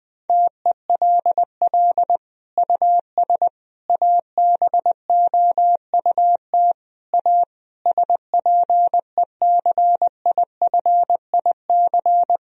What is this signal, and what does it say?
Morse code, heard as a keyed tone, about 20 wpm, 700 Hz. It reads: TELL US ABOUT A SPECIFIC
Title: Sentences from Top 1000 Words - Review Part 3 - 20wpm